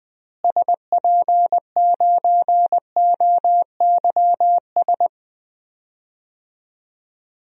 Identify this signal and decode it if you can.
Morse code, heard as a keyed tone, about 20 wpm, 700 Hz. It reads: SP9OYS
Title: DXCC-100 Call Signs - 6 Characters - Repeated 2x - 20wpm